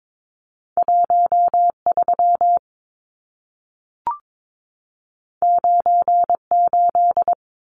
Morse code, heard as a keyed tone, about 22 wpm, 700 Hz. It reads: 13 98
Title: Numbers - 2-Digit 22wpm